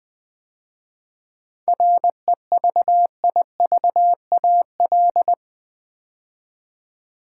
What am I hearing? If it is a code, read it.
Morse code, heard as a keyed tone, about 20 wpm, 690 Hz. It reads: REVIVAL
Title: 7 Letter Words - First 3 Letters Given - 20wpm